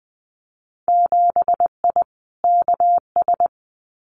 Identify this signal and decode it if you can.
Morse code, heard as a keyed tone, about 20 wpm, 700 Hz. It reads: 7I KS
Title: Winter Field Day Exchanges Encore 20wpm